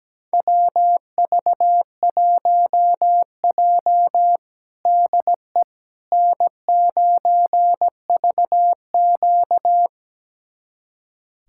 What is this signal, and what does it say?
Morse code, heard as a keyed tone, about 17 wpm, 700 Hz. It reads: WV1J DE N9VQ
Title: USA Callsign DE Callsign - 4 Characters - 17wpm